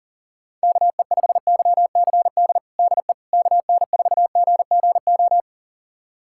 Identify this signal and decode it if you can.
Morse code, heard as a keyed tone, about 40 wpm, 700 Hz. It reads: KE5YCD DE KN4GGO
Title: USA Callsign DE Callsign - 6 Characters - 40wpm